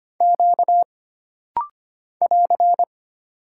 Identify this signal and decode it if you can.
Morse code, heard as a keyed tone, about 25 wpm, 700 Hz. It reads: Q <AR>
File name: Single Character (Letter:Number:Punctuation:Prosign) - Rapid-Fire - 25wpm